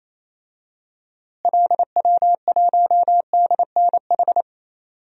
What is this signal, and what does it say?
Morse code, heard as a keyed tone, about 28 wpm, 700 Hz. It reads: LW1DNH
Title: DXCC-100 Call Signs - 6 Characters - Repeated 2x - 28wpm